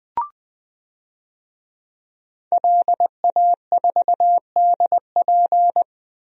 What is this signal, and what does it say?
Morse code, heard as a keyed tone, about 21 wpm, 700 Hz. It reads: LA4DP
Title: DXCC-100 Call Signs - 5 Characters - Repeated 2x - 20wpm